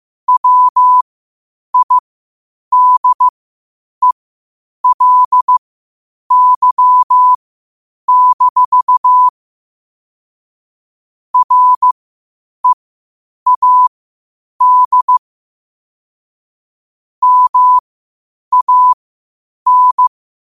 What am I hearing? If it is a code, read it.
Morse code, heard as a keyed tone, about 15 wpm, spaced as for 8 wpm, 1000 Hz. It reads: WIDELY-READ MAN
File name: qotdfiles / 8wpm